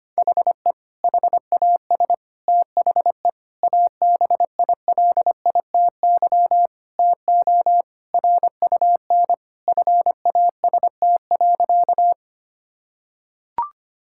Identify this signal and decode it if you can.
Morse code, heard as a keyed tone, about 25 wpm, 700 Hz. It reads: HE HAS THE ABILITY TO RUN FAST.
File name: practice / 25wpm / Sentences with 3-Word Phrases - 25wpm